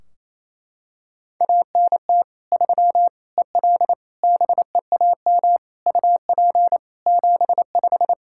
Morse code, heard as a keyed tone, about 28 wpm, 700 Hz. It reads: ANT 3 EL BEAM UP 75
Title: Practice QSO - Exchange 2 - Info - 28wpm